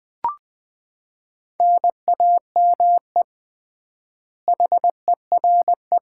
Morse code, heard as a keyed tone, about 20 wpm, 700 Hz. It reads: NAME HERE
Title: Common QSO Phrases - 3x Word-Spacing - 20wpm